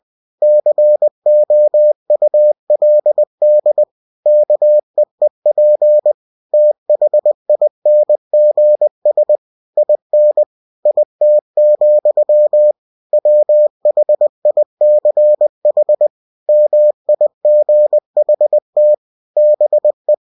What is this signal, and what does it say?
Morse code, heard as a keyed tone, about 20 wpm, 600 Hz. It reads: COULD KEEP THINGS IN IT, WHICH MIGHT BE
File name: cw-books / winnie-the-pooh / winnie-20wpm-page-0050